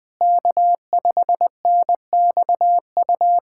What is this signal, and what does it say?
Morse code, heard as a keyed tone, about 20 wpm, 700 Hz. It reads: K5NXU
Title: USA Callsign DE Callsign - 5 Characters - 20wpm